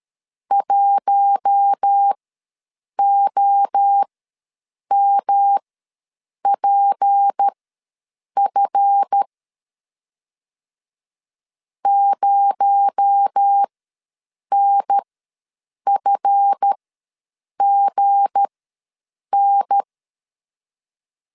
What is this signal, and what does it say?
Morse code, heard as a keyed tone, about 13 wpm, spaced as for 7 wpm, 800 Hz. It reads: 1OMPF 0NFGN
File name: LEZIONI / 7 GF9Z / 7 GF ws 12wpm 3140